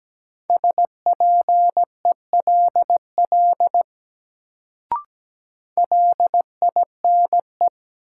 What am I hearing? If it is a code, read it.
Morse code, heard as a keyed tone, about 17 wpm, 700 Hz. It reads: SPELL LINE
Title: Sentences from Top 1000 Words - Review Part 1 - New Word Drill - 17wpm